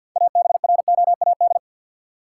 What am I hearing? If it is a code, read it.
Morse code, heard as a keyed tone, about 49 wpm, 700 Hz. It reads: ABROAD
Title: Six Letter Words - 50wpm